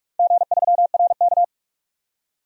Morse code, heard as a keyed tone, about 46 wpm, 700 Hz. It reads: G3RK
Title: DXCC-100 Call Signs - 4 Characters - 45wpm